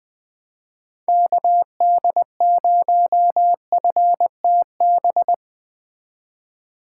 Morse code, heard as a keyed tone, about 20 wpm, 700 Hz. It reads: KD0FTB